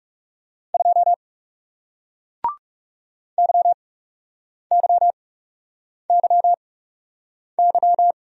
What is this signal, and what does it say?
Morse code, heard as a keyed tone, about 40 wpm, spaced as for 8 wpm, 700 Hz. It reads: 2 YYYY